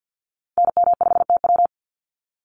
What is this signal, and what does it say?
Morse code, heard as a keyed tone, about 48 wpm, 700 Hz. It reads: NM5TW
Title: NA SOTA Activator Callsigns 50wpm